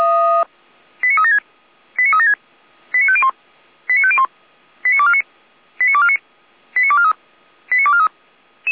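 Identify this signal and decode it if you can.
Five-tone selective call (ZVEI), heard as quick runs of five tones, tones of about 70 ms. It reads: 89376 89376 89501 89501 89249 89249 89243 89243
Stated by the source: ZVEI five-tone selective call received by SDR, from the audio archive of the feuerwehr-orlamuende.org fire brigade.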